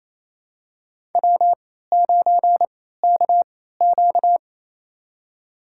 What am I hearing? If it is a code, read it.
Morse code, heard as a keyed tone, about 28 wpm, spaced as for 16 wpm, 700 Hz. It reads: W9KQ
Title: USA Call Signs - 4 Characters - 3x Character Spacing - 28wpm